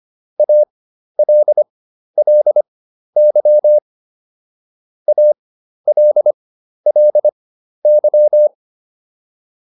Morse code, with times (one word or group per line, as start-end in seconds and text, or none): 0.4-3.8 ALLY
5.1-8.5 ALLY